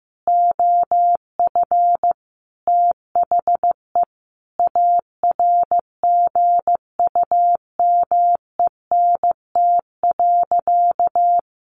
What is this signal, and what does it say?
Morse code, heard as a keyed tone, about 15 wpm, 700 Hz. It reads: OF THE ARGUMENT.
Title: Sentences from Top 1400 Words 15wpm